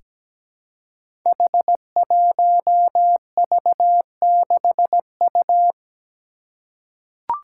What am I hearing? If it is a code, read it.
Morse code, heard as a keyed tone, about 17 wpm, 700 Hz. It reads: H1V6U
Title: Five Challenging Letter-Number Combinations 17wpm